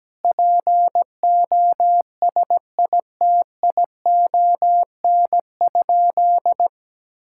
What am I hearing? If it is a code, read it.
Morse code, heard as a keyed tone, about 17 wpm, 700 Hz. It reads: POSITION?
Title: Sentences from Top 1000 Words - Review Part 5 - 17wpm